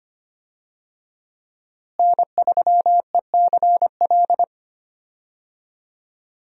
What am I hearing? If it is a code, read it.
Morse code, heard as a keyed tone, about 25 wpm, 700 Hz. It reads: N3ECL
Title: USA Call Signs - 5 Characters - Repeated 3x - 25wpm